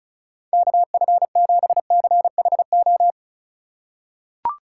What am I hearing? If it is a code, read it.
Morse code, heard as a keyed tone, about 36 wpm, 700 Hz. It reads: KF7CHO
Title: USA Call Signs - 6 Characters - 3x Character Spacing - 35wpm